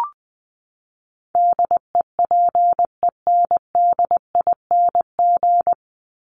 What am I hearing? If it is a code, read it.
Morse code, heard as a keyed tone, about 20 wpm, 700 Hz. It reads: DEPENDING
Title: ING Suffix - Top 100 Words with Base Words 20wpm